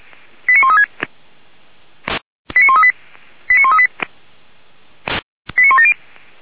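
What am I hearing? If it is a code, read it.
Five-tone selective call (ZVEI), heard as quick runs of five tones, tones of about 70 ms. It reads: 89136 89138 89138 89170
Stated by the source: ZVEI five-tone call received by SDR, from the audio archive of the Feuerwehr Orlamünde fire brigade.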